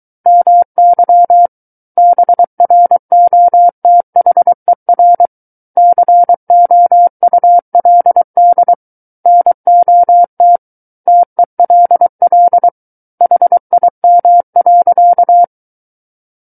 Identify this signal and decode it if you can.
Morse code, heard as a keyed tone, about 23 wpm, 700 Hz. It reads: MY BROTHER COULD NOT TELL HIM.
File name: Book One / 14-Chapter 14 - 23 WPM